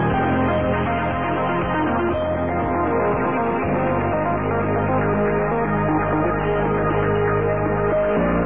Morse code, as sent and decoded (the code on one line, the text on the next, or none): none